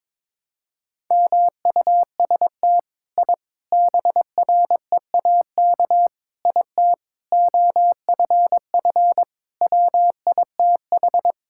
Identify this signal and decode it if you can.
Morse code, heard as a keyed tone, about 22 wpm, 700 Hz. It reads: MUST I BREAK IT OFF WITH
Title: Sentences from Top 1000 Words - Review Part 6 - 22wpm